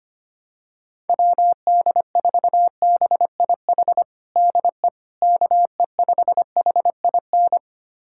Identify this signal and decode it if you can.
Morse code, heard as a keyed tone, about 25 wpm, 700 Hz. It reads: WD4BIH DE KE5HIN
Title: USA Callsign DE Callsign 25wpm